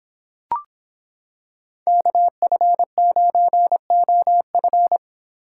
Morse code, heard as a keyed tone, about 26 wpm, 700 Hz. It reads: KF9OF